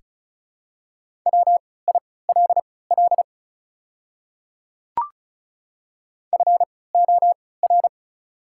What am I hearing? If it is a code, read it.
Morse code, heard as a keyed tone, about 36 wpm, spaced as for 20 wpm, 690 Hz. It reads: WILL FOR
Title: 3x Character Spacing - Top 100 Words - 35wpm